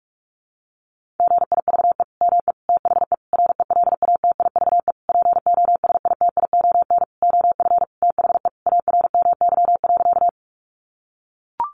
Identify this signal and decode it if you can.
Morse code, heard as a keyed tone, about 44 wpm, 700 Hz. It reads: GIVE ME THE RELATIVE POSITION OF THE ARMY.
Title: Sentences from Top 1400 Words 45wpm